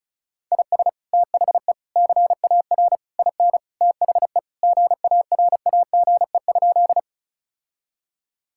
Morse code, heard as a keyed tone, about 36 wpm, 700 Hz. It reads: IS THE CAR IN THE GARAGE?
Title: Sentences from Top 1600 Words 35wpm